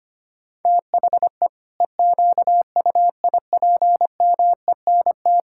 Morse code, heard as a keyed tone, about 25 wpm, 700 Hz. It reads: THE EQUIPMENT
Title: Sentences from Top 1000 Words - Review Part 6 - 25wpm